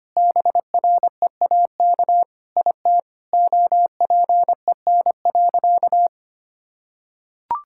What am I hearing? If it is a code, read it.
Morse code, heard as a keyed tone, about 25 wpm, 700 Hz. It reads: BREAK IT OPEN.
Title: Sentences from Top 1000 Words - Review Part 6 - 25wpm